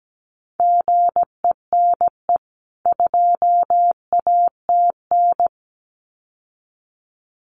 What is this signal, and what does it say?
Morse code, heard as a keyed tone, about 17 wpm, 700 Hz. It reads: GENE 2ATN
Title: CWT Exchange 17wpm